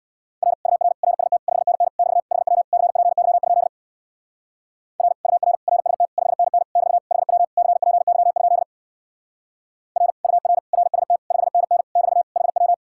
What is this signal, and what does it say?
Morse code, heard as a keyed tone, about 61 wpm, 700 Hz. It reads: R UR RST 5NN <BT> HW CPY? R UR RST 5NN <BT> HW CPY? R UR RST 5NN <BT> HW
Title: Speed-Racing Common QSO Phrases Encore - 50wpm